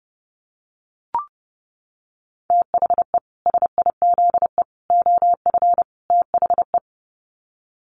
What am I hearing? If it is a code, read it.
Morse code, heard as a keyed tone, about 30 wpm, 700 Hz. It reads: THE SIZE OF THE
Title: Sets of 4 Words 30wpm